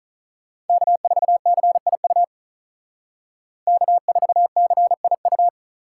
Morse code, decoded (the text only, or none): K4CIU K4CIU